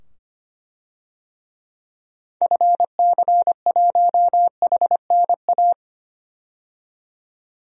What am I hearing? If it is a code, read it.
Morse code, heard as a keyed tone, about 25 wpm, 700 Hz. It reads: FC1HNA
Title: DXCC-100 Call Signs - 6 Characters - 25wpm